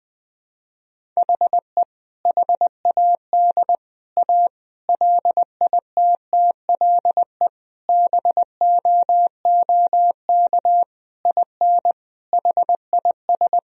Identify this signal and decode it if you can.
Morse code, heard as a keyed tone, about 20 wpm, 700 Hz. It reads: HE HAD A LITTLE BOOK IN HIS
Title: Sentences from Top 1000 Words - Review Part 1 - 20wpm